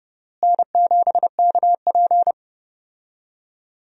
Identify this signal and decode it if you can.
Morse code, heard as a keyed tone, about 32 wpm, 700 Hz. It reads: N7KP
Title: USA Call Signs - 4 Characters - Repeated 2x - 30wpm